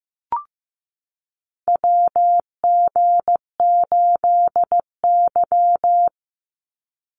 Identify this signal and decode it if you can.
Morse code, heard as a keyed tone, about 15 wpm, 700 Hz. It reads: WG8Y
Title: NA SOTA Chaser Callsigns 15wpm